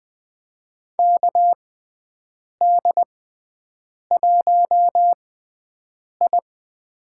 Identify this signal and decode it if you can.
Morse code, heard as a keyed tone, about 20 wpm, spaced as for 7 wpm, 700 Hz. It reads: KD1I
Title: USA Call Signs - 4 Characters - 6x Character Spacing - 20wpm